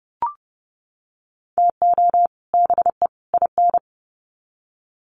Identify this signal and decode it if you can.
Morse code, heard as a keyed tone, about 30 wpm, 700 Hz. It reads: TO BE IN
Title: Sets of 3 Words 30wpm